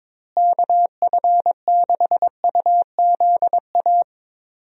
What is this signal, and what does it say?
Morse code, heard as a keyed tone, about 22 wpm, 700 Hz. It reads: KF6UZA
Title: USA Call Signs - 6 Characters - Repeated 3x - 22wpm